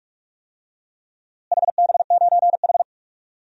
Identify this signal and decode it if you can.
Morse code, heard as a keyed tone, about 46 wpm, 690 Hz. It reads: HB9H
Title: DXCC-100 Call Signs - 4 Characters - Repeated 3x - 45wpm